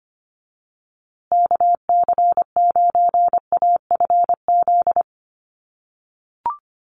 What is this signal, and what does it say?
Morse code, heard as a keyed tone, about 25 wpm, 700 Hz. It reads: KC9AFZ
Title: USA Call Signs Repeated 3x 25wpm